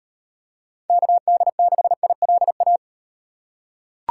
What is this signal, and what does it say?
Morse code, heard as a keyed tone, about 38 wpm, 700 Hz. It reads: KD6ILA